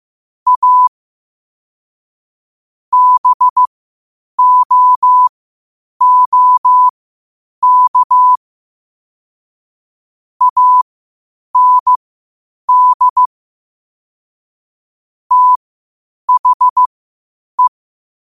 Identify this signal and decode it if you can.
Morse code, heard as a keyed tone, about 15 wpm, spaced as for 8 wpm, 1000 Hz. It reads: A BOOK AND THE
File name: 8wpm